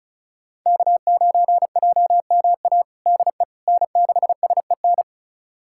Morse code, heard as a keyed tone, about 36 wpm, 700 Hz. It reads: K9JMA DE N6SEN
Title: USA Callsign DE Callsign - 5 Characters - 35wpm